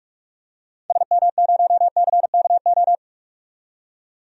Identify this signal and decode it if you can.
Morse code, heard as a keyed tone, about 46 wpm, 700 Hz. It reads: SM0CKO
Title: DXCC-100 Call Signs - 6 Characters - 45wpm